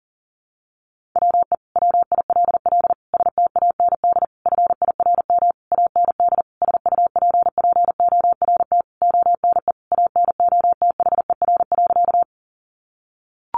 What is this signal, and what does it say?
Morse code, heard as a keyed tone, about 40 wpm, 700 Hz. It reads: WE WILL STAND FIRM AND SUPPORT ONE ANOTHER.
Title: Sentences from Top 1400 Words 40wpm